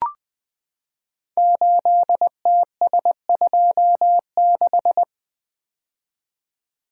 Morse code, heard as a keyed tone, about 20 wpm, 700 Hz. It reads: 8TS26
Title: Code Groups - 5 Letter:Numbers - 20wpm